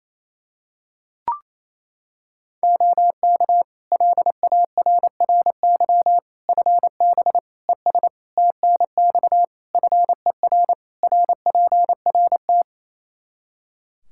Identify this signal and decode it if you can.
Morse code, heard as a keyed tone, about 28 wpm, 700 Hz. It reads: OK LARRY FB ES TNX FER RPRT
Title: Practice QSO - Exchange 2 - Info - 28wpm